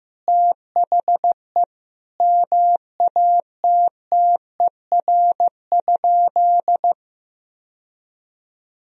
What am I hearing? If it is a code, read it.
Morse code, heard as a keyed tone, about 15 wpm, 700 Hz. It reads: THE MATTER?
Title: Sentences from Top 1000 Words - Review Part 4 - 15wpm